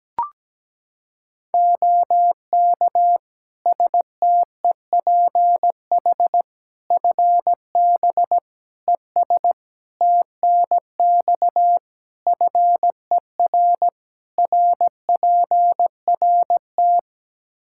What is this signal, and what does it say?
Morse code, heard as a keyed tone, about 17 wpm, 700 Hz. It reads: OK STEPH FB ES TNX FER RPRT
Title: Practice QSO - Exchange 2 - Info - 17wpm